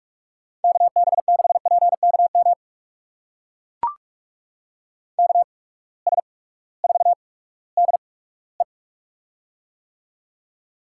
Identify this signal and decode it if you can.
Morse code, heard as a keyed tone, about 42 wpm, spaced as for 20 wpm, 700 Hz. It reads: KB6PKM KS4DE
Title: USA Call Signs - 8x Character Spacing - 45wpm